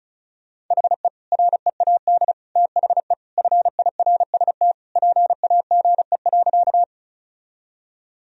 Morse code, heard as a keyed tone, about 36 wpm, 700 Hz. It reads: HE READ THE FIRST PAGE.
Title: Sentences from Top 1000 Words - Review Part 1 - 35wpm